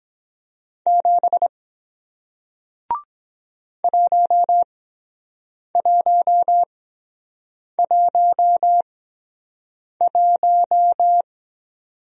Morse code, heard as a keyed tone, about 22 wpm, spaced as for 7 wpm, 700 Hz. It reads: 7 1111